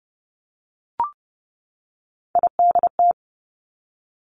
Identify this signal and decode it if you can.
Morse code, heard as a keyed tone, about 30 wpm, 700 Hz. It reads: IDT
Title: Three Challenging Letter-Number Combinations 30wpm